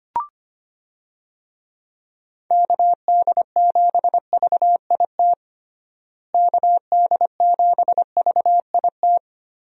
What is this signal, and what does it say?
Morse code, heard as a keyed tone, about 25 wpm, 700 Hz. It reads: KD7VIT KD7VIT
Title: USA Call Signs - 6 Characters - Repeated 2x - 25wpm